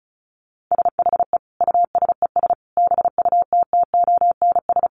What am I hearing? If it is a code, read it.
Morse code, heard as a keyed tone, about 34 wpm, 700 Hz. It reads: SHE USES BUTTONS
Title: Easy Phrases 35wpm